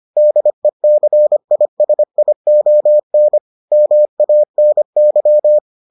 Morse code, heard as a keyed tone, about 25 wpm, 600 Hz. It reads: DECISION MANY